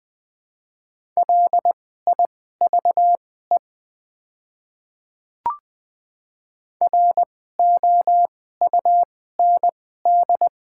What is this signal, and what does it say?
Morse code, heard as a keyed tone, about 20 wpm, spaced as for 14 wpm, 700 Hz. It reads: LIVE ROUND
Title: 2x Character Spacing - Top 300 Words - 20wpm